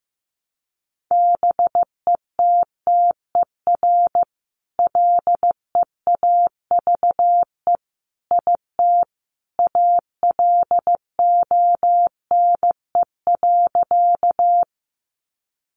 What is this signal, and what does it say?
Morse code, heard as a keyed tone, about 14 wpm, 700 Hz. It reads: BETTER LEAVE IT ALONE.
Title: Sentences from Top 1000 Words 15wpm